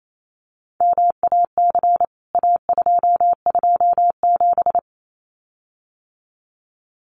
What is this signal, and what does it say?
Morse code, heard as a keyed tone, about 28 wpm, 700 Hz. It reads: MAC A227
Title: CWT Exchange 28wpm